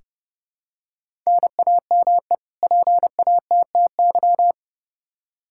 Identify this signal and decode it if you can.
Morse code, heard as a keyed tone, about 30 wpm, 700 Hz. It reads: NAME PATTY
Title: K1USN SST QSO Practice 30wpm